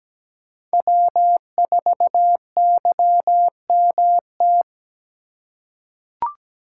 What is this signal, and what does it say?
Morse code, heard as a keyed tone, about 17 wpm, 700 Hz. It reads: W4YMT
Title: USA Call Signs - 3x Character Spacing - 17wpm